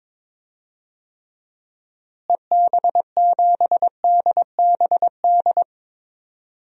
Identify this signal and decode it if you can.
Morse code, heard as a keyed tone, about 22 wpm, 700 Hz. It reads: EB7DBD
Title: DXCC-100 Call Signs - 6 Characters - Repeated 3x - 22wpm